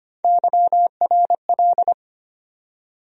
Morse code, heard as a keyed tone, about 25 wpm, 700 Hz. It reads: YRL
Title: USA Call Sign Postfixes - 3 Letters 25wpm